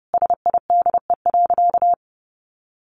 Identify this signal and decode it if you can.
Morse code, heard as a keyed tone, about 30 wpm, 700 Hz. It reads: SIDE.
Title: Sentences from Top 400 Words 30wpm